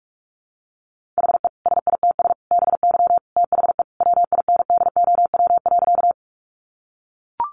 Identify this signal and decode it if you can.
Morse code, heard as a keyed tone, about 44 wpm, 700 Hz. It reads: HE SITS BY THE WINDOW.
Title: Easy Phrases 45wpm